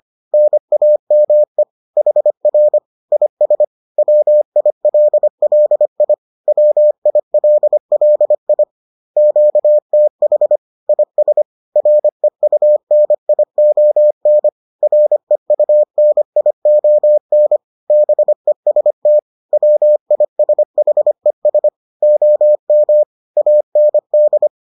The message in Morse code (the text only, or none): NAME HR IS WILLI WILLI QTH IS REUNION REUNION BEST WISHES OM AND